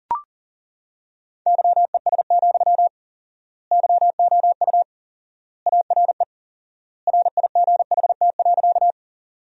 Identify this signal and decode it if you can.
Morse code, heard as a keyed tone, about 40 wpm, 700 Hz. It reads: YES, YOU ARE RIGHT.
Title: 4x Word-Spacing - Sentences from Top 500 Words - 40wpm